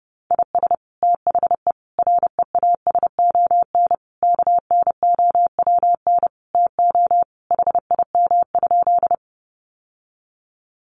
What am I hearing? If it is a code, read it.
Morse code, heard as a keyed tone, about 30 wpm, 700 Hz. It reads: IS THE REASON KNOWN TO HIM?
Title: Sentences from Top 700 Words 30wpm